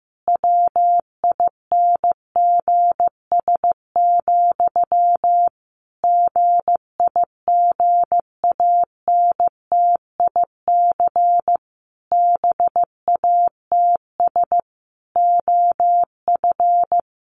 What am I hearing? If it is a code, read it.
Morse code, heard as a keyed tone, about 15 wpm, 700 Hz. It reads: WINGS, GIGANTIC BATS OF
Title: A Sound Of Thunder 15wpm